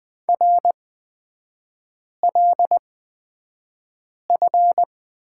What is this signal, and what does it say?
Morse code, heard as a keyed tone, about 20 wpm, spaced as for 5 wpm, 700 Hz. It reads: RLF
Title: 243 Lesson 39e - Letter z - Three Characters 20wpm